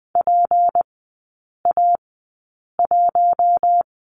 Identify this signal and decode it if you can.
Morse code, heard as a keyed tone, about 20 wpm, spaced as for 8 wpm, 700 Hz. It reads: PA1
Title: 152 Lesson 25e - Letter p - Three Characters 20wpm